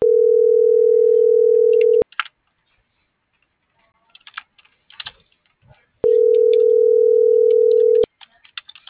Ringback tone (North America), 440 + 480 Hz.
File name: CENTRAL SOTOVA / out alejandro.rodriguez - .52106